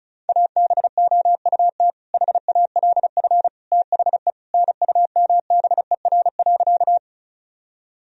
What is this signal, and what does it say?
Morse code, heard as a keyed tone, about 36 wpm, 700 Hz. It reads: ABOUT HALF THE NUMBER.
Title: Sentences from Top 1000 Words - Review Part 1 - 35wpm